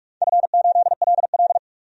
Morse code, heard as a keyed tone, about 44 wpm, 700 Hz. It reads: F8LL
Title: DXCC-100 Call Signs - 4 Characters - 45wpm